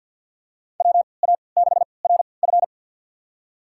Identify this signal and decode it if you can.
Morse code, heard as a keyed tone, about 52 wpm, spaced as for 29 wpm, 700 Hz. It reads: WA6RF
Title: USA Call Signs - 5 Characters - 3x Character Spacing - 50wpm